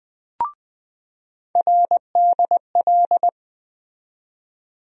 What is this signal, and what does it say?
Morse code, heard as a keyed tone, about 20 wpm, 700 Hz. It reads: RDL
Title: USA Call Sign Postfixes - 3 Letters 20wpm